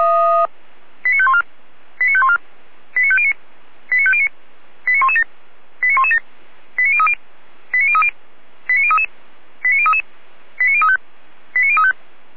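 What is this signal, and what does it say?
Five-tone selective call (ZVEI), heard as quick runs of five tones, tones of about 70 ms. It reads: 89514 89514 89509 89509 89117 89117 89030 89030 89033 89033 89035 89035
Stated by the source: Five-tone ZVEI selective call received on SDR.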